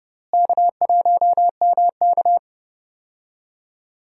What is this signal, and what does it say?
Morse code, heard as a keyed tone, about 30 wpm, 700 Hz. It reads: K1MK
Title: USA Call Signs - 4 Characters - Repeated 3x - 30wpm